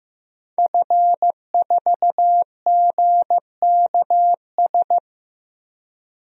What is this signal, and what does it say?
Morse code, heard as a keyed tone, about 15 wpm, 700 Hz. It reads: F4GKS
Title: DXCC-100 Call Signs - 5 Characters - Repeated 2x - 15wpm